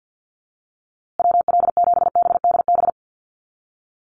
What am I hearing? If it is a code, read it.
Morse code, heard as a keyed tone, about 50 wpm, 700 Hz. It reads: WL7BDB
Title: USA Call Signs Repeated 2x 50wpm